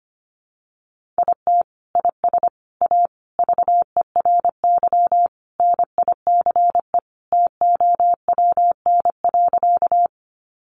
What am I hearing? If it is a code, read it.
Morse code, heard as a keyed tone, about 25 wpm, 700 Hz. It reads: IT IS A VERY NICE TOWN.
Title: Sentences from Top 1100 Words 25wpm